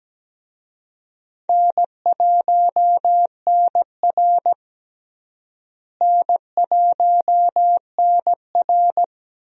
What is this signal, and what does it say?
Morse code, heard as a keyed tone, about 17 wpm, 700 Hz. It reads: N1NR N1NR
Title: USA Call Signs - 4 Characters - Repeated 2x - 17wpm